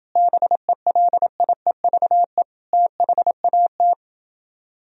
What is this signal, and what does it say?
Morse code, heard as a keyed tone, about 31 wpm, 700 Hz. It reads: BELIEVE THAT